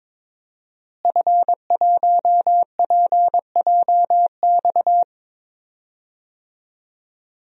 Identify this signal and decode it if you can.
Morse code, heard as a keyed tone, about 22 wpm, 700 Hz. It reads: F1PJX
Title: DXCC-100 Call Signs - 5 Characters - Repeated 3x - 22wpm